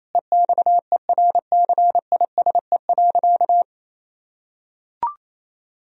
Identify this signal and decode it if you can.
Morse code, heard as a keyed tone, about 28 wpm, 700 Hz. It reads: EXERCISE.
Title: Sentences from Top 1000 Words - Review Part 4 - 28wpm